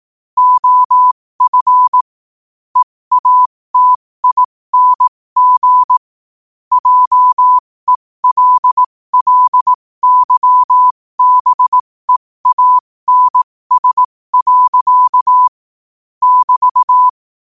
Morse code, heard as a keyed tone, about 18 wpm, spaced as for 15 wpm, 1000 Hz. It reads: OF EATING JELLYBEANS. <BT>
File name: qotdfiles / 15wpm